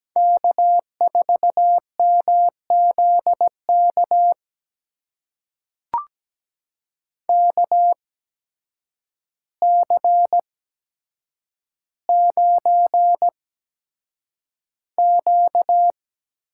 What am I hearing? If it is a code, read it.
Morse code, heard as a keyed tone, about 17 wpm, spaced as for 8 wpm, 700 Hz. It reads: K4MZK KC9Q
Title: USA Call Signs - 8x Character Spacing - 17wpm